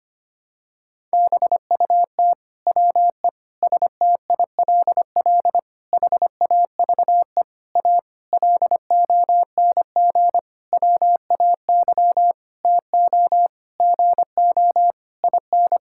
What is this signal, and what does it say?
Morse code, heard as a keyed tone, about 25 wpm, 700 Hz. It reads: BUT WE STILL HAVE A LONG WAY TO GO IN
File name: practice / 25wpm / Sentences with 9-Word Phrases - 25wpm